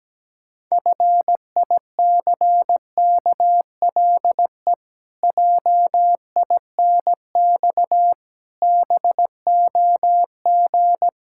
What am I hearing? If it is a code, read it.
Morse code, heard as a keyed tone, about 17 wpm, 700 Hz. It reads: FICKLE JINX BOG
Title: Pangram Challenge v2 - 17wpm